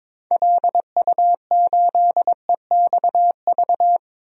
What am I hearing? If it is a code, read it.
Morse code, heard as a keyed tone, about 22 wpm, 700 Hz. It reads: LU8EXV
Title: DXCC-100 Call Signs - 6 Characters - 22wpm